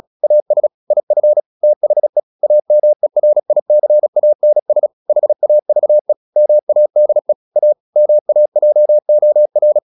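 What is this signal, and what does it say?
Morse code, heard as a keyed tone, about 36 wpm, 600 Hz. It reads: AS IF THE AMERICANS HAVE MADE A MAJOR
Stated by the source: Morse practice at 36 wpm, text of Ken Follett's novel Never, from CW med Gnister.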